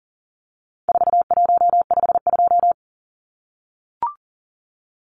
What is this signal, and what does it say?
Morse code, heard as a keyed tone, about 40 wpm, 700 Hz. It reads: 4152
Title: Numbers - 4-Digit 40wpm